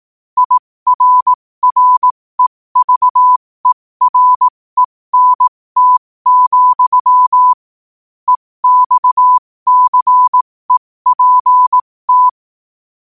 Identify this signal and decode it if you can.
Morse code, heard as a keyed tone, about 18 wpm, spaced as for 15 wpm, 1000 Hz. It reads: IRREVERENT, EXCEPT
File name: 15wpm